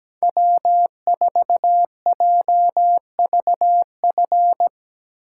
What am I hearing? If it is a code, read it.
Morse code, heard as a keyed tone, about 18 wpm, 700 Hz. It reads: W4JVF
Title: Speed Racing - USA Call Signs 17wpm